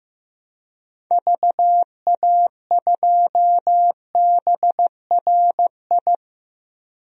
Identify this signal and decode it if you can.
Morse code, heard as a keyed tone, about 16 wpm, 700 Hz. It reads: VA2BRI
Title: DXCC-100 Call Signs - 6 Characters - 15wpm